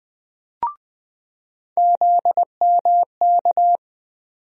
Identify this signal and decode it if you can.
Morse code, heard as a keyed tone, about 20 wpm, 700 Hz. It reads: ZMK
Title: USA Call Sign Postfixes - 3 Letters 20wpm